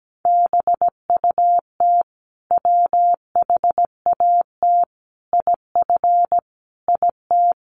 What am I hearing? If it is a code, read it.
Morse code, heard as a keyed tone, about 17 wpm, 700 Hz. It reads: BUT WHAT IF IT
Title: Sentences from Top 100 Words 17wpm